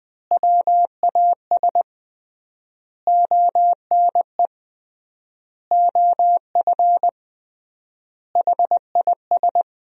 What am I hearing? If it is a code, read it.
Morse code, heard as a keyed tone, about 21 wpm, 700 Hz. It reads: WAS ONE OF HIS
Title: 3x Word-Spacing - Sentences from Top 300 Words - 20wpm